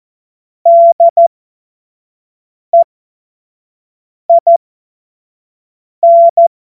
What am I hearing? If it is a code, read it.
Morse code, heard as a keyed tone, about 14 wpm, spaced as for 5 wpm, 680 Hz. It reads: DEIN
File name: Morsekurs / Witze / Witz 14. Februar 2023 25 F70